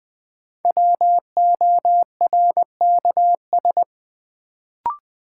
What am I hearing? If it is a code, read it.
Morse code, heard as a keyed tone, about 20 wpm, 700 Hz. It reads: WORKS
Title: Sentences from Top 1000 Words - Review Part 5 - New Word Drill - 20wpm